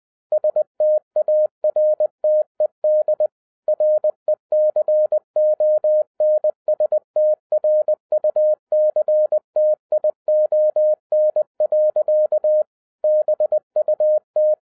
Morse code, heard as a keyed tone, about 20 wpm, 610 Hz. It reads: STARTED RECONSTRUCTION. BUT
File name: 2011-10-mp3 / n03-TARO JR0QWW 20wpm